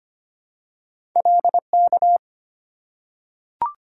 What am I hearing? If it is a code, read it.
Morse code, heard as a keyed tone, about 25 wpm, 700 Hz. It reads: LK
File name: Two Letter-Number Combinations - 25wpm